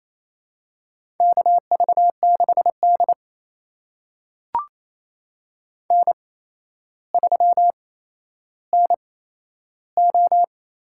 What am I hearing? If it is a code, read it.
Morse code, heard as a keyed tone, about 28 wpm, spaced as for 12 wpm, 700 Hz. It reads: KV6D N3NO
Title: USA Call Signs - 4 Characters - 8x Character Spacing - 28wpm